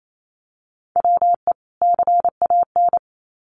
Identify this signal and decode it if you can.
Morse code, heard as a keyed tone, about 28 wpm, 700 Hz. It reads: WE CAN
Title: Sets of 2 Words 28wpm